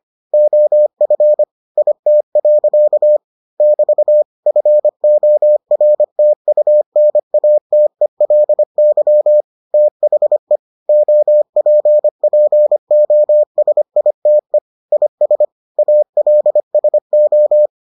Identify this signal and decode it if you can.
Morse code, heard as a keyed tone, about 25 wpm, 600 Hz. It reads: OF IT. <BT> FORTUNATELY THE OPPOSITE IS ALSO